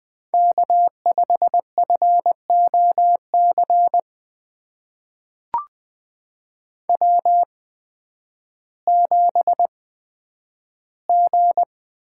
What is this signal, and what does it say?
Morse code, heard as a keyed tone, about 20 wpm, spaced as for 11 wpm, 700 Hz. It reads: K5FOC W7G
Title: USA Call Signs - 5 Characters - 8x Character Spacing - 20wpm